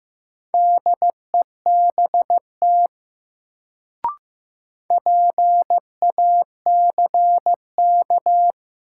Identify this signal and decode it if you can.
Morse code, heard as a keyed tone, about 15 wpm, 700 Hz. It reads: DEBT PACK
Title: Sentences from Top 1300 Words - New Word Drill Alt - 15wpm